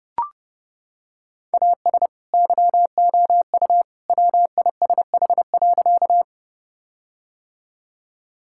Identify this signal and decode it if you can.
Morse code, heard as a keyed tone, about 30 wpm, 700 Hz. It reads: AS YOU WISH.
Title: Sentences from Top 1600 Words 30wpm